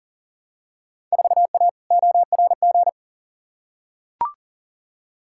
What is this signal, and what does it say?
Morse code, heard as a keyed tone, about 40 wpm, 700 Hz. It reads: 4A ORG
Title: Field Day Exchanges - 40wpm